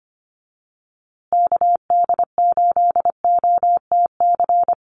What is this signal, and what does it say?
Morse code, heard as a keyed tone, about 25 wpm, 700 Hz. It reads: KD8OTC
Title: USA Call Signs Repeated 3x 25wpm